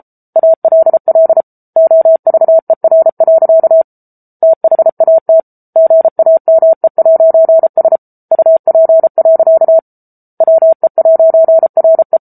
Morse code, heard as a keyed tone, about 34 wpm, 650 Hz. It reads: ALL OVER. THAT GAME'S UP. WE'RE